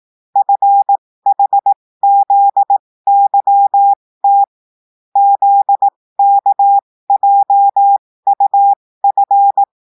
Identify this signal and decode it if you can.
Morse code, heard as a keyed tone, about 18 wpm, spaced as for 15 wpm, 800 Hz. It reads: FHZYT ZKJUF
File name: cours-du-lundi files / cours-1580-texte-chiffre-15-mpm